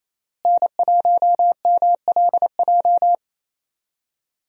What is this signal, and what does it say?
Morse code, heard as a keyed tone, about 27 wpm, 700 Hz. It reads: N1MLJ
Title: USA Call Signs - 5 Characters - Repeated 3x - 28wpm